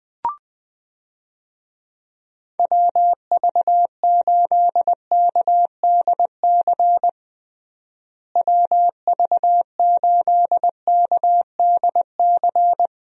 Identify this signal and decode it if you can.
Morse code, heard as a keyed tone, about 20 wpm, 700 Hz. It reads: WV8KDC WV8KDC
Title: USA Call Signs - 6 Characters - Repeated 2x - 20wpm